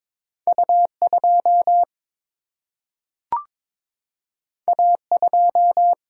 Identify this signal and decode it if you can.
Morse code, heard as a keyed tone, about 22 wpm, 700 Hz. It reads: U2 A2
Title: Single-Letter Single-Number 22wpm